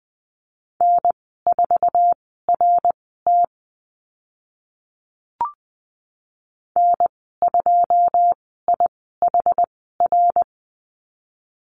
Morse code, heard as a keyed tone, about 20 wpm, spaced as for 14 wpm, 700 Hz. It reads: N4RT N2IHR
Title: 082 Lesson 14g - Number 2 - Call Signs 20wpm